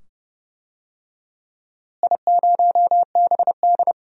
Morse code, heard as a keyed tone, about 30 wpm, 700 Hz. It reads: I0BD
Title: DXCC-100 Call Signs - 4 Characters - Repeated 2x - 30wpm